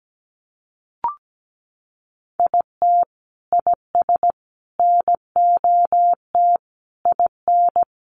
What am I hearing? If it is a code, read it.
Morse code, heard as a keyed tone, about 17 wpm, 700 Hz. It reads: IT IS NOT IN
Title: Sentences from Top 1200 Words 17wpm